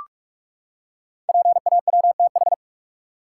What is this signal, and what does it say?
Morse code, heard as a keyed tone, about 44 wpm, 700 Hz. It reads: PAWTH
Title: Code Groups - 5 Letters - 45wpm